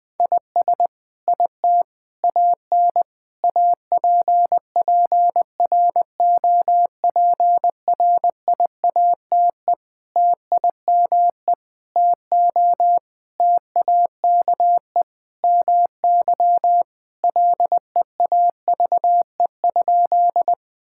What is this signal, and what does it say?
Morse code, heard as a keyed tone, about 20 wpm, 700 Hz. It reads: IS IT AN APPROPRIATE TIME TO TAKE MY LEAVE?
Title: Sentences from Top 1000 Words - Review Part 5 - 20wpm